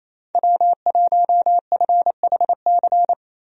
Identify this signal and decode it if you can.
Morse code, heard as a keyed tone, about 28 wpm, 700 Hz. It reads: W1FHC
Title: USA Call Signs - 5 Characters - Repeated 3x - 28wpm